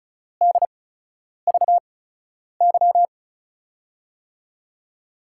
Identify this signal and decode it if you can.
Morse code, heard as a keyed tone, about 36 wpm, spaced as for 10 wpm, 700 Hz. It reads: DVY
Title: USA Call Signs - 8x Character Spacing - 35wpm